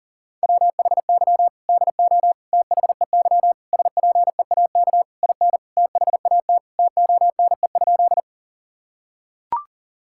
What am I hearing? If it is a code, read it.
Morse code, heard as a keyed tone, about 40 wpm, 700 Hz. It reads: WHY DO THEY SPEAK IN THAT TONE?
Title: Sentences from Top 1600 Words 40wpm